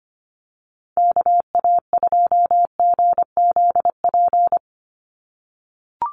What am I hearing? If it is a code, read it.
Morse code, heard as a keyed tone, about 25 wpm, 700 Hz. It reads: KA2GZP
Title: USA Call Signs 25wpm